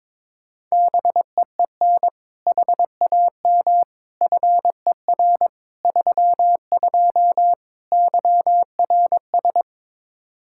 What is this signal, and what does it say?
Morse code, heard as a keyed tone, about 22 wpm, 700 Hz. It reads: BEEN HAM FER 32 YRS
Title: K1USN SST QSO Practice 22wpm